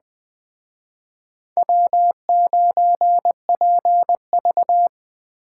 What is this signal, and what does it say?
Morse code, heard as a keyed tone, about 20 wpm, 700 Hz. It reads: W9PV
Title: Speed Racing - USA 1x2 Call Signs 17wpm